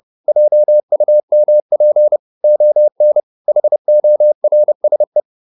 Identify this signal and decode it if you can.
Morse code, heard as a keyed tone, about 30 wpm, 600 Hz. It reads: JUMP ON HORSE